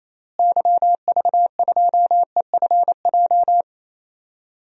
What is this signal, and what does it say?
Morse code, heard as a keyed tone, about 28 wpm, 700 Hz. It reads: YV2EFJ
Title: DXCC-100 Call Signs - 6 Characters - Repeated 2x - 28wpm